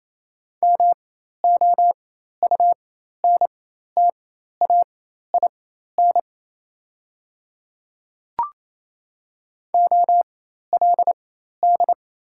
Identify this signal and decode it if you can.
Morse code, heard as a keyed tone, about 28 wpm, spaced as for 13 wpm, 700 Hz. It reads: MOUNTAIN OLD
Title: 4x Character Spacing - Top 300 Words - 28wpm